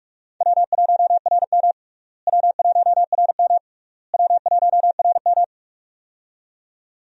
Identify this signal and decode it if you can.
Morse code, heard as a keyed tone, about 46 wpm, 700 Hz. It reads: W1RM W1RM W1RM
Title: USA Call Signs - 4 Characters - Repeated 3x - 45wpm